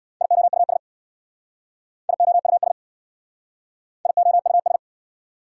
Morse code, heard as a keyed tone, about 65 wpm, 690 Hz. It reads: IOUS IOUS IOUS